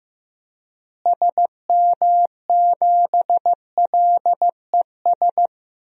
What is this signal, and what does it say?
Morse code, heard as a keyed tone, about 15 wpm, 700 Hz. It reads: SM7LES
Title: DXCC-100 Call Signs - 6 Characters - Repeated 2x - 15wpm